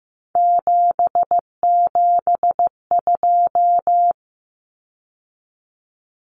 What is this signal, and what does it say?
Morse code, heard as a keyed tone, about 15 wpm, 700 Hz. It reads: 772
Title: Numbers - 3-Digit 15wpm